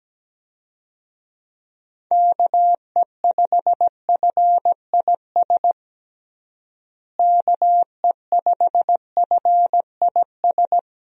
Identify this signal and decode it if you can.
Morse code, heard as a keyed tone, about 17 wpm, 700 Hz. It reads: KE5FIS KE5FIS
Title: USA Call Signs - 6 Characters - Repeated 2x - 17wpm